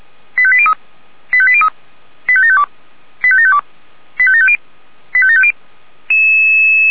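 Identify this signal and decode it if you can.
Five-tone selective call (ZVEI), heard as quick runs of five tones, tones of about 70 ms. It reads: 85803 85803 85752 85752 85750 85750
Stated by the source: ZVEI five-tone call received by SDR.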